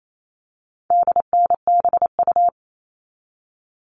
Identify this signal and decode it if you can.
Morse code, heard as a keyed tone, about 28 wpm, 700 Hz. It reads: DNBU
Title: All Cut Numbers - 4-Digit 28wpm